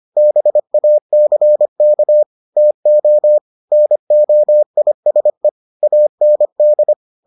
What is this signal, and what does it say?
Morse code, heard as a keyed tone, about 25 wpm, 600 Hz. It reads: BACK TO NOISE AND